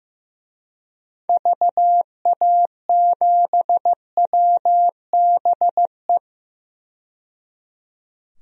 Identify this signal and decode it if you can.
Morse code, heard as a keyed tone, about 16 wpm, 700 Hz. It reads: VA7WBE
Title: DXCC-100 Call Signs - 6 Characters - 15wpm